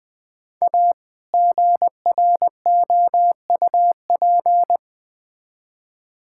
Morse code, heard as a keyed tone, about 20 wpm, 700 Hz. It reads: A GROUP
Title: Build A Phrase - 3-Word Phrases - 20wpm